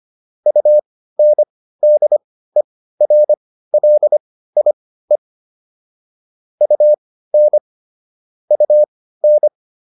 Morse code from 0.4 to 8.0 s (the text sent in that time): UNDERLIE UN